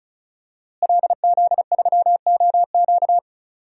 Morse code, heard as a keyed tone, about 36 wpm, 700 Hz. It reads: LZ3OQ
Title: DXCC-100 Call Signs - 5 Characters - Repeated 3x - 35wpm